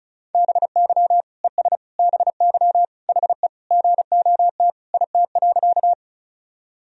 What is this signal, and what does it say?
Morse code, heard as a keyed tone, about 36 wpm, 700 Hz. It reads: BY ES BY HE GOT IT.
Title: Sentences from Top 300 Words and Abbreviations 35wpm